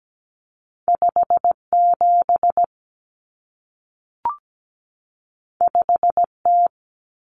Morse code, heard as a keyed tone, about 17 wpm, 700 Hz. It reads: 57 5T
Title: Common Cut Numbers - 2-Digit 17wpm